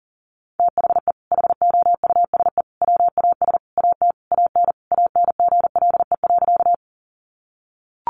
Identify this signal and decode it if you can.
Morse code, heard as a keyed tone, about 40 wpm, 700 Hz. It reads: THE HOUSE WAS AT AN ANGLE.
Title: Sentences from Top 1500 Words 40wpm